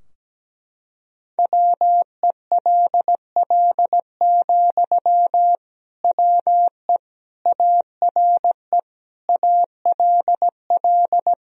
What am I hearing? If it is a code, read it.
Morse code, heard as a keyed tone, about 17 wpm, 700 Hz. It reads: WELL, WE ARE ALL
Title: Sentences from Top 1000 Words - Review Part 4 - 17wpm